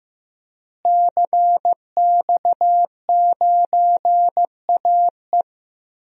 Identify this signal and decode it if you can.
Morse code, heard as a keyed tone, about 15 wpm, 700 Hz. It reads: CX9AE
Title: DXCC-100 Call Signs - 5 Characters - Repeated 2x - 15wpm